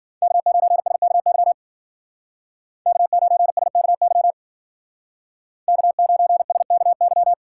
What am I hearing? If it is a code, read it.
Morse code, heard as a keyed tone, about 52 wpm, 700 Hz. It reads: K9SKY K9SKY K9SKY